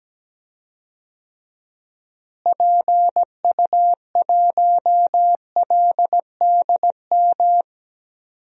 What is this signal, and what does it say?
Morse code, heard as a keyed tone, about 17 wpm, 700 Hz. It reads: PU1LDM
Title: DXCC-100 Call Signs - 6 Characters - Repeated 3x - 17wpm